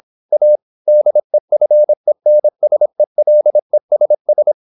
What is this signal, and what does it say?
Morse code, heard as a keyed tone, about 26 wpm, 600 Hz. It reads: A DEFENSELESS
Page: Never 26wpm | CW med Gnister